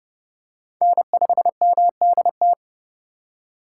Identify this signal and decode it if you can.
Morse code, heard as a keyed tone, about 32 wpm, 700 Hz. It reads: N5MDT
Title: USA Call Signs - 5 Characters - 8x Character Spacing - 30wpm